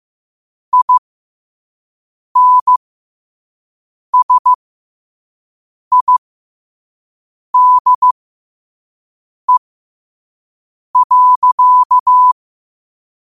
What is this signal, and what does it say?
Morse code, heard as a keyed tone, about 15 wpm, spaced as for 5 wpm, 1000 Hz. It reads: INSIDE.
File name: qotdfiles / 5wpm